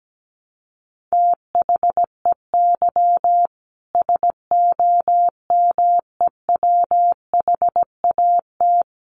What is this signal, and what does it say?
Morse code, heard as a keyed tone, about 17 wpm, 700 Hz. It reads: THEY SOMEWHAT
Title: Sentences from Top 1300 Words 17wpm